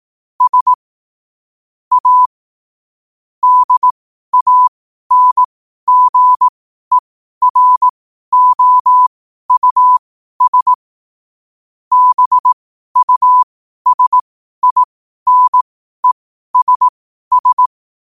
Morse code, heard as a keyed tone, about 18 wpm, spaced as for 12 wpm, 1000 Hz. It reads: S A DANGEROUS BUSINESS
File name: qotdfiles / 12wpm